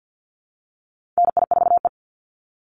Morse code, heard as a keyed tone, about 50 wpm, 700 Hz. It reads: NI4E
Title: Call Signs - 50wpm